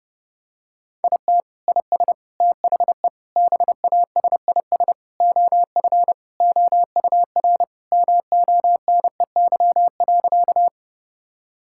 Morse code, heard as a keyed tone, about 30 wpm, 700 Hz. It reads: IT IS THE BASIS OF OUR MONEY.
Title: Sentences from Top 1000 Words - Review Part 5 - 30wpm